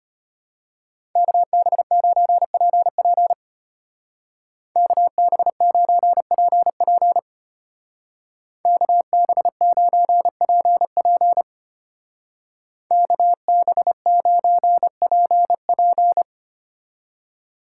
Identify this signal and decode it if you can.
Morse code, heard as a keyed tone, about 32 wpm, 700 Hz. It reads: KB9PP KB9PP KB9PP KB9PP